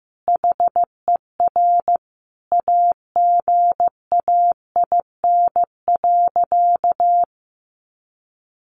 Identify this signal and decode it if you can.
Morse code, heard as a keyed tone, about 15 wpm, 700 Hz. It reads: HER AGAIN.
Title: Sentences from Top 1500 Words 15wpm